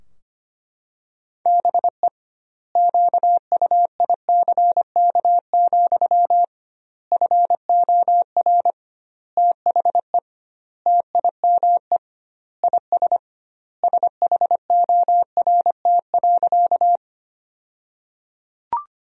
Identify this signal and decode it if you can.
Morse code, heard as a keyed tone, about 25 wpm, 700 Hz. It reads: BE QUICK, FOR THE TIME IS SHORT.
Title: 2x Word-Spacing - Sentences from Top 500 Words - 25wpm